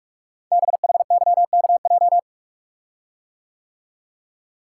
Morse code, heard as a keyed tone, about 44 wpm, 700 Hz. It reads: BHYKJ
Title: Code Groups - 5 Letters - 45wpm